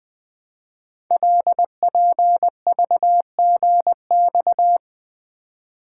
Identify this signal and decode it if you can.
Morse code, heard as a keyed tone, about 20 wpm, 700 Hz. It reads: LPVGX
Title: Code Groups - 5 Letters - 20wpm